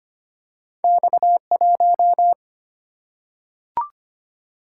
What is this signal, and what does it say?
Morse code, heard as a keyed tone, about 25 wpm, 700 Hz. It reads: X1
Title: Single-Letter Single-Number 25wpm